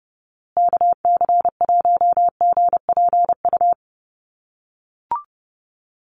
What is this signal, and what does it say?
Morse code, heard as a keyed tone, about 30 wpm, 700 Hz. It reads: KC1GPU
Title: USA Call Signs Repeated 2x 30wpm